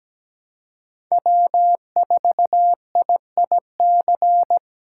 Morse code, heard as a keyed tone, about 17 wpm, 700 Hz. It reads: W4IIC
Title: K1USN SST QSO Practice - 17wpm